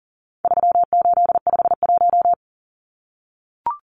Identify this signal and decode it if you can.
Morse code, heard as a keyed tone, about 40 wpm, 700 Hz. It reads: 3851
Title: Numbers - 4-Digit 40wpm